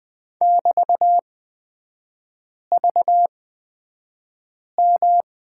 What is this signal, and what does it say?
Morse code, heard as a keyed tone, about 20 wpm, spaced as for 5 wpm, 700 Hz. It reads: <BT>VM
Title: 236 Lesson 38e - Letter q - Three Characters 20wpm